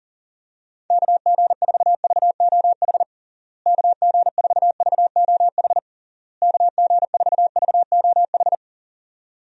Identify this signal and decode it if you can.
Morse code, heard as a keyed tone, about 40 wpm, 700 Hz. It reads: KG4VOH KG4VOH KG4VOH
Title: USA Call Signs - 6 Characters - Repeated 3x - 40wpm